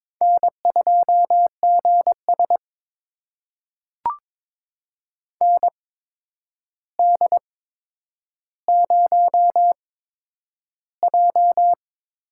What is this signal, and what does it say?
Morse code, heard as a keyed tone, about 22 wpm, spaced as for 9 wpm, 700 Hz. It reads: N2GS ND0J
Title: USA Call Signs - 4 Characters - 8x Character Spacing - 22wpm